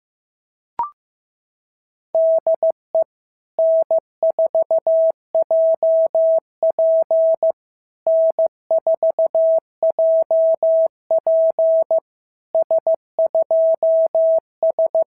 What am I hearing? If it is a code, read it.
Morse code, heard as a keyed tone, about 14 wpm, 650 Hz. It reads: DE N4JP N4JP S2S
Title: SOTA QSO Practice 15wpm